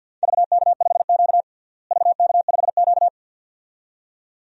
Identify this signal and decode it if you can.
Morse code, heard as a keyed tone, about 52 wpm, 700 Hz. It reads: VK5Q VK5Q
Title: DXCC-100 Call Signs - 4 Characters - Repeated 2x - 50wpm